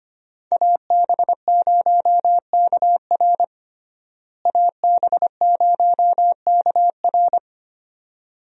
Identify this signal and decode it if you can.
Morse code, heard as a keyed tone, about 25 wpm, 700 Hz. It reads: AB0KR AB0KR
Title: USA Call Signs - 5 Characters - Repeated 2x - 25wpm